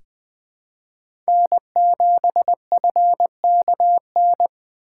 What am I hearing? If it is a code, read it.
Morse code, heard as a keyed tone, about 20 wpm, 700 Hz. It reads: N7FKN
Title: USA Call Signs - 5 Characters - 4x Character Spacing - 20wpm